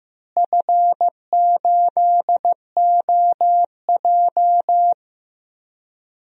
Morse code, heard as a keyed tone, about 15 wpm, 700 Hz. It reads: F8OJ
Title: DXCC-100 Call Signs - 4 Characters - Repeated 3x - 15wpm